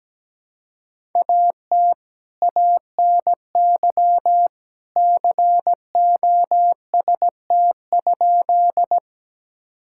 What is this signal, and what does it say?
Morse code, heard as a keyed tone, about 17 wpm, 700 Hz. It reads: AT ANY COST?
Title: Sentences from Top 1000 Words - Review Part 5 - 17wpm